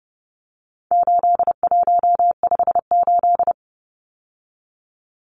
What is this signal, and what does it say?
Morse code, heard as a keyed tone, about 30 wpm, 700 Hz. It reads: 8158
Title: Numbers - 4-Digit 30wpm